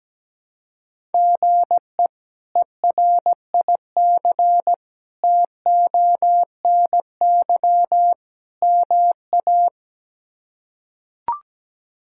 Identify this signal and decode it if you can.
Morse code, heard as a keyed tone, about 17 wpm, 700 Hz. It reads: GE ERIC TONY MA
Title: K1USN SST QSO Practice - 17wpm